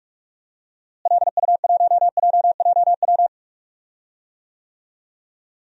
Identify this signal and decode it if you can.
Morse code, heard as a keyed tone, about 46 wpm, 700 Hz. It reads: LU1JJW
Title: DXCC-100 Call Signs - 6 Characters - Repeated 3x - 45wpm